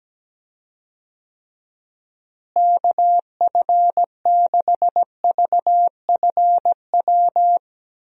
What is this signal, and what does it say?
Morse code, heard as a keyed tone, about 17 wpm, 700 Hz. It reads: KF6VFW
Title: USA Call Signs - 6 Characters - Repeated 3x - 17wpm